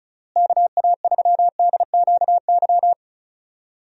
Morse code, heard as a keyed tone, about 34 wpm, 700 Hz. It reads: KA3DQY
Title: USA Call Signs - 6 Characters - Repeated 3x - 35wpm